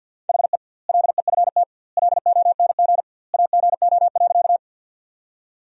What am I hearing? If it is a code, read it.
Morse code, heard as a keyed tone, about 52 wpm, 700 Hz. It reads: HE LEFT LONG AGO.
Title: 2x Word-Spacing - Sentences from Top 500 Words - 50wpm